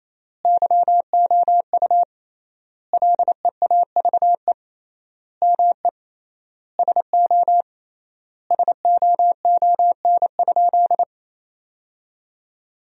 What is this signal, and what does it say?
Morse code, heard as a keyed tone, about 28 wpm, 700 Hz. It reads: YOU LEAVE ME SO SOON?
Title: 3x Word-Spacing - Sentences from Top 400 Words - 28wpm